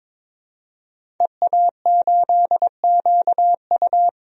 Morse code, heard as a keyed tone, about 22 wpm, 700 Hz. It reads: EA8QU
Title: DXCC-100 Call Signs - 5 Characters - Repeated 2x - 22wpm